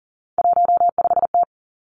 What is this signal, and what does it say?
Morse code, heard as a keyed tone, about 38 wpm, 700 Hz. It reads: 15T
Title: Three Challenging Letter-Number Combinations 40wpm